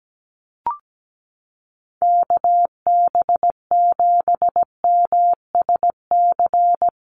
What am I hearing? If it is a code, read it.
Morse code, heard as a keyed tone, about 17 wpm, 700 Hz. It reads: KB7MSC
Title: USA Call Signs Repeated 2x 17wpm